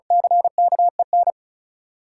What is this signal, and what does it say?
Morse code, heard as a keyed tone, about 36 wpm, 700 Hz. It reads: CKEN
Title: Sentences from Top 1000 Words - Review Part 5 - New Word Drill - 35wpm